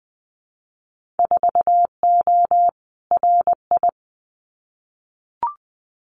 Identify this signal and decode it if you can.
Morse code, heard as a keyed tone, about 20 wpm, 700 Hz. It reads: 4O RI
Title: Winter Field Day Exchanges Encore 20wpm